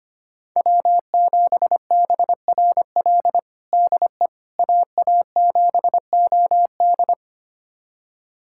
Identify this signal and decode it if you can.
Morse code, heard as a keyed tone, about 25 wpm, 700 Hz. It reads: W7BRL DE AA7OD
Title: USA Callsign DE Callsign - 5 Characters - 25wpm